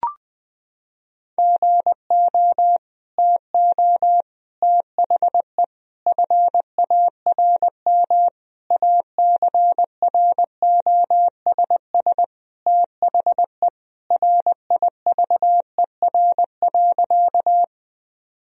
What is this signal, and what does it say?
Morse code, heard as a keyed tone, about 20 wpm, 700 Hz. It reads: GO TO THE FARM ACROSS THE RIVER.
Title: Sentences from Top 1000 Words - Review Part 2 - 20wpm